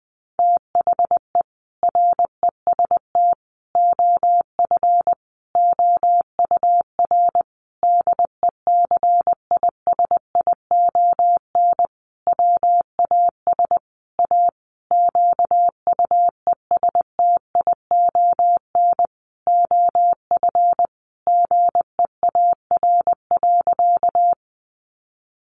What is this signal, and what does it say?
Morse code, heard as a keyed tone, about 20 wpm, 700 Hz. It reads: THE REST OF OUR DECISION WAS A QUESTION OF GEAR.
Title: Sentences from Top 1400 Words 20wpm